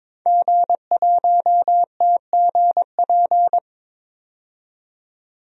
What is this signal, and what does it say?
Morse code, heard as a keyed tone, about 23 wpm, 700 Hz. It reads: G1TGP
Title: DXCC-100 Call Signs - 5 Characters - 22wpm